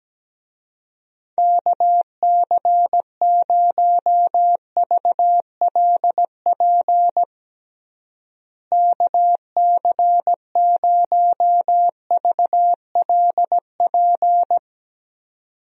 Morse code, heard as a keyed tone, about 17 wpm, 700 Hz. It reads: KC0VLP KC0VLP
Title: USA Call Signs - 6 Characters - Repeated 2x - 17wpm